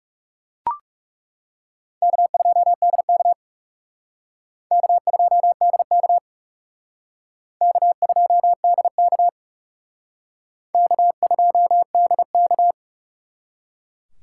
Morse code, heard as a keyed tone, about 38 wpm, 700 Hz. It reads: K2DK K2DK K2DK K2DK